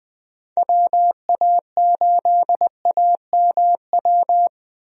Morse code, heard as a keyed tone, about 20 wpm, 700 Hz. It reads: WA8AMW
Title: USA Call Signs - 6 Characters - Repeated 3x - 20wpm